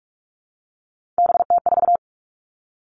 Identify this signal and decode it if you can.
Morse code, heard as a keyed tone, about 46 wpm, 700 Hz. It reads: BT4
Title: Three Challenging Letter-Number Combinations 45wpm